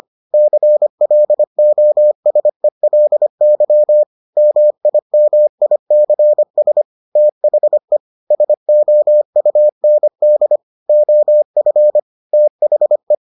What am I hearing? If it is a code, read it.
Morse code, heard as a keyed tone, about 25 wpm, 600 Hz. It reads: CLOSELY MIMICS THE SOUND OF THE